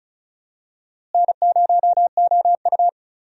Morse code, heard as a keyed tone, about 36 wpm, 700 Hz. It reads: N0OU
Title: USA Call Signs - 4 Characters - Repeated 3x - 35wpm